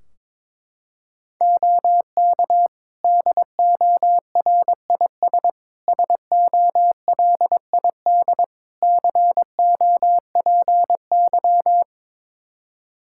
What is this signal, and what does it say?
Morse code, heard as a keyed tone, about 22 wpm, 700 Hz. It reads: OK DORIS SOLID COPY
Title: Practice QSO - Exchange 3 - More Info - 22wpm